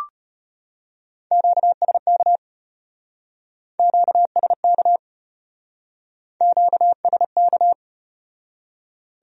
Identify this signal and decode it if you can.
Morse code, heard as a keyed tone, about 33 wpm, 700 Hz. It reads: QSK QSK QSK